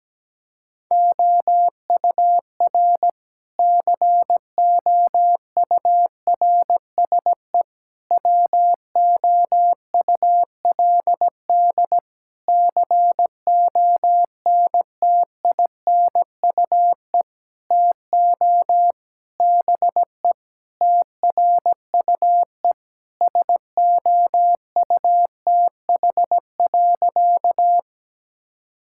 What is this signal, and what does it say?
Morse code, heard as a keyed tone, about 17 wpm, 700 Hz. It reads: OUR COURSE WOULD CONTINUE TO BE TRUE SOUTH.
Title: Sentences from Top 1000 Words - Review Part 6 - 17wpm